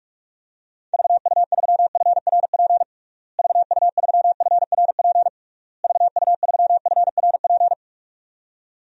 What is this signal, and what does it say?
Morse code, heard as a keyed tone, about 46 wpm, 700 Hz. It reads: VU3FRP VU3FRP VU3FRP
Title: DXCC-100 Call Signs - 6 Characters - Repeated 3x - 45wpm